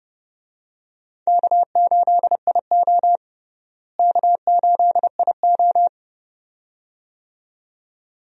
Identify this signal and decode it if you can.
Morse code, heard as a keyed tone, about 30 wpm, 700 Hz. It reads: K8IO K8IO
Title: USA Call Signs - 4 Characters - Repeated 2x - 30wpm